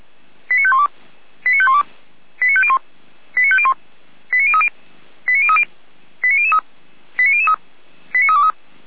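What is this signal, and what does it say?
Five-tone selective call (ZVEI), heard as quick runs of five tones, tones of about 70 ms. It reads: 89512 89512 89501 89501 89030 89030 89003 89003 89323
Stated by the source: ZVEI five-tone selective call received via SDR.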